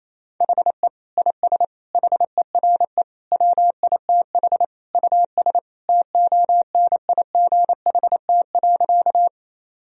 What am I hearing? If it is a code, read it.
Morse code, heard as a keyed tone, about 28 wpm, 700 Hz. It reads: HE IS HERE WITH US TONIGHT.
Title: Sentences from Top 1000 Words - Review Part 3 - 28wpm